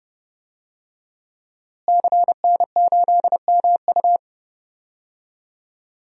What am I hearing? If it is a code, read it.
Morse code, heard as a keyed tone, about 31 wpm, 700 Hz. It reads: CN8MU
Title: DXCC-100 Call Signs - 5 Characters - Repeated 3x - 30wpm